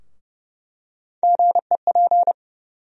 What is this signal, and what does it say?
Morse code, heard as a keyed tone, about 31 wpm, 700 Hz. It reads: GEP
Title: USA Call Sign Postfixes - 3 Letters 30wpm